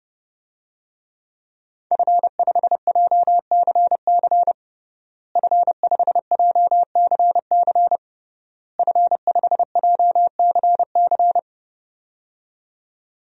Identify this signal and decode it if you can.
Morse code, heard as a keyed tone, about 30 wpm, 700 Hz. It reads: F5JCC F5JCC F5JCC
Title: DXCC-100 Call Signs - 5 Characters - Repeated 3x - 30wpm